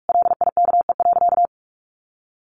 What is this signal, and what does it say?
Morse code, heard as a keyed tone, about 44 wpm, 700 Hz. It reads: LIKE.
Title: Sentences from Top 1400 Words 45wpm